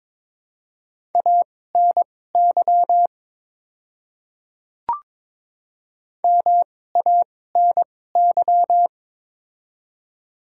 Morse code, heard as a keyed tone, about 21 wpm, spaced as for 16 wpm, 700 Hz. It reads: ANY MANY
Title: 2x Character Spacing - Top 200 Words - 22wpm